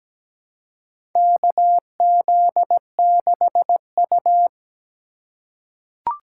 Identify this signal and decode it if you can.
Morse code, heard as a keyed tone, about 17 wpm, 700 Hz. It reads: KZ6U
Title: USA Call Signs - 4x Character Spacing - 17wpm